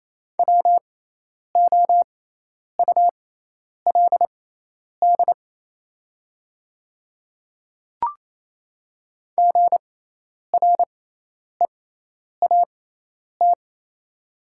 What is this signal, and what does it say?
Morse code, heard as a keyed tone, about 28 wpm, spaced as for 10 wpm, 700 Hz. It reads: WOULD GREAT
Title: 6x Character Spacing - Top 200 Words - 28wpm